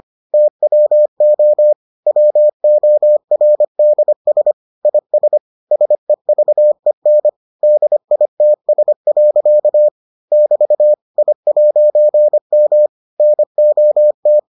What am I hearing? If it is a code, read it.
Morse code, heard as a keyed tone, about 25 wpm, 600 Hz. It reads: TWO WORDS IS SEVEN DITS. <BT> I'M NOT